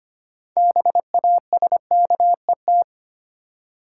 Morse code, heard as a keyed tone, about 25 wpm, 700 Hz. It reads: BASKET
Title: Six Letter Words - 25wpm